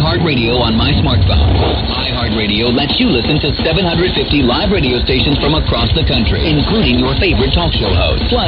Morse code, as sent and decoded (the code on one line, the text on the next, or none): none